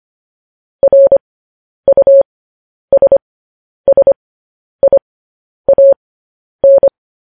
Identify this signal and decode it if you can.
Morse code, heard as a keyed tone, about 25 wpm, spaced as for 10 wpm, 550 Hz. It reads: RUSSIAN
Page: news headlines in morse code at 10 wpm